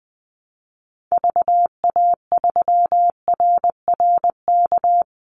Morse code, heard as a keyed tone, about 20 wpm, 700 Hz. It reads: VA3RRK
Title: Call Signs - 20wpm